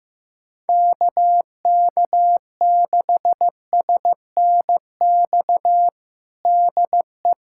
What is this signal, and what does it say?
Morse code, heard as a keyed tone, about 15 wpm, 700 Hz. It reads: KK6SNX DE
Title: USA Callsign DE Callsign - 6 Characters - 15wpm